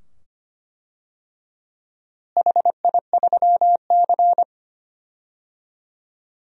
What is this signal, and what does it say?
Morse code, heard as a keyed tone, about 25 wpm, 700 Hz. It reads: HI3C
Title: DXCC-100 Call Signs - 4 Characters - Repeated 2x - 25wpm